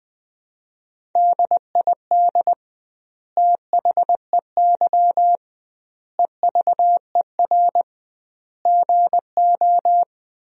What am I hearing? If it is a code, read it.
Morse code, heard as a keyed tone, about 20 wpm, 700 Hz. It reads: DID THEY EVER GO
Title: 2x Word-Spacing - Sentences from Top 400 Words - 20wpm